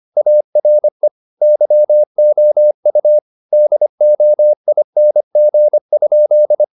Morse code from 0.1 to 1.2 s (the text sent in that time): ARE